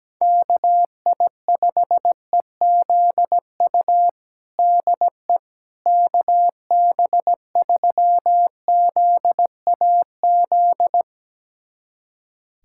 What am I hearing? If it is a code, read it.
Morse code, heard as a keyed tone, about 17 wpm, 700 Hz. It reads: KI5EZU DE KB3ZAZ
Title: USA Callsign DE Callsign - 6 Characters - 17wpm